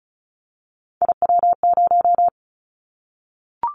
Morse code, heard as a keyed tone, about 36 wpm, 700 Hz. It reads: IW0
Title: Callsign Prefixes for Top 50 DXCC - 35wpm